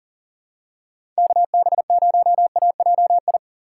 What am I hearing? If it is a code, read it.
Morse code, heard as a keyed tone, about 40 wpm, 700 Hz. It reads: KB0AJI
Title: USA Call Signs - 6 Characters - 2x Character Spacing - 40wpm